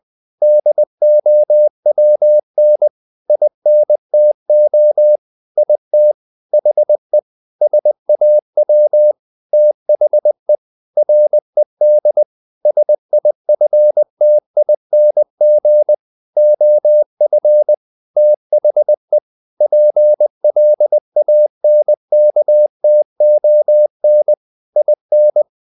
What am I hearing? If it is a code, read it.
Morse code, heard as a keyed tone, about 20 wpm, 600 Hz. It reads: DOWN INTO IT HE SAW THE RED SIFTING OF THE PLANKTON IN
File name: the old man and the sea / the old man and the sea part 1 20wpm